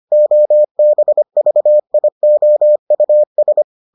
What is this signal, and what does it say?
Morse code, heard as a keyed tone, about 25 wpm, 600 Hz. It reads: OBVIOUS